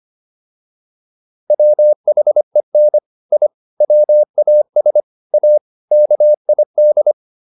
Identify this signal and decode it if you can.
Morse code, heard as a keyed tone, about 25 wpm, 600 Hz. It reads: WHEN I WAS A KID